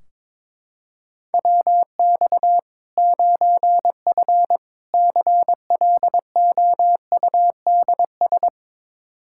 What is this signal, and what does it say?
Morse code, heard as a keyed tone, about 22 wpm, 700 Hz. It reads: WX 9F CLOUDS
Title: K1USN SST QSO Practice 22wpm